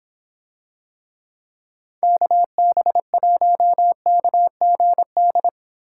Morse code, heard as a keyed tone, about 25 wpm, 700 Hz. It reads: KB1KGD